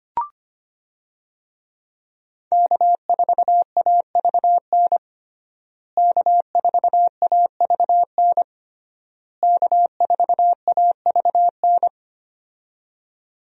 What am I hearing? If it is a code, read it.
Morse code, heard as a keyed tone, about 25 wpm, 700 Hz. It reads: K4AVN K4AVN K4AVN
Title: USA Call Signs - 5 Characters - Repeated 3x - 25wpm